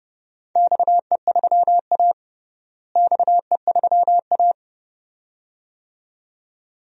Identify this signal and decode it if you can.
Morse code, heard as a keyed tone, about 30 wpm, 700 Hz. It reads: XE3A XE3A
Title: DXCC-100 Call Signs - 4 Characters - Repeated 2x - 30wpm